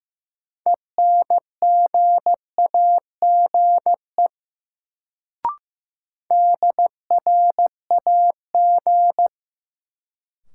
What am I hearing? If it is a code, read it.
Morse code, heard as a keyed tone, about 15 wpm, 700 Hz. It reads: ENGAGE DRAG
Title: Sentences from Top 1500 Words - New Word Drill Alt - 15wpm